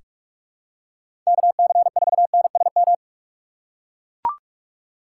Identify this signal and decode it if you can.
Morse code, heard as a keyed tone, about 44 wpm, 700 Hz. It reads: KC4NSM
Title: USA Call Signs - 6 Characters - 6x Character Spacing - 45wpm